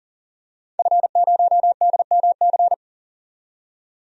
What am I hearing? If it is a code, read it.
Morse code, heard as a keyed tone, about 40 wpm, 700 Hz. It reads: F0DMC
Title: DXCC-100 Call Signs - 5 Characters - 40wpm